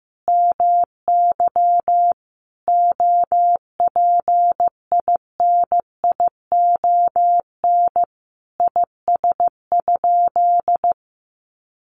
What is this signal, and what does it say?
Morse code, heard as a keyed tone, about 15 wpm, 700 Hz. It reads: MY OPINION IS?
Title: Sentences from Top 1200 Words 15wpm